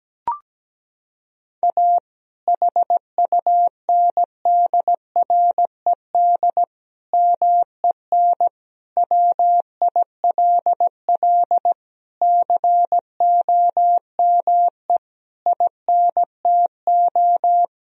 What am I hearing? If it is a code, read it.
Morse code, heard as a keyed tone, about 17 wpm, 700 Hz. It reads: A HUNDRED MEN WILL COME INTO
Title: Sentences from Top 1000 Words - Review Part 2 - 17wpm